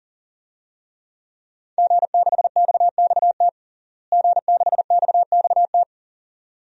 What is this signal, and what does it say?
Morse code, heard as a keyed tone, about 40 wpm, 700 Hz. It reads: G6XXT G6XXT
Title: DXCC-100 Call Signs - 5 Characters - Repeated 2x - 40wpm